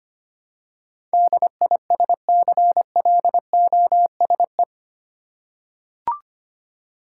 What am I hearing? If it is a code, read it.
Morse code, heard as a keyed tone, about 25 wpm, 700 Hz. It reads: DISCLOSE
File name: Eight Letter Words - 25wpm